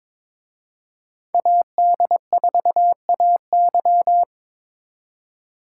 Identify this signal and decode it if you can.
Morse code, heard as a keyed tone, about 26 wpm, 700 Hz. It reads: AD4AY
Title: Speed Racing - USA 2x2 Call Signs 22wpm